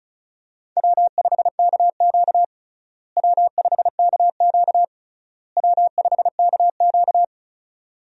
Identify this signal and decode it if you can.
Morse code, heard as a keyed tone, about 36 wpm, 700 Hz. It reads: W5KQ W5KQ W5KQ
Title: USA Call Signs - 4 Characters - Repeated 3x - 35wpm